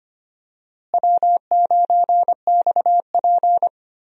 Morse code, heard as a keyed tone, about 25 wpm, 700 Hz. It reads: W9XP
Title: USA Call Signs - 4 Characters - Repeated 3x - 25wpm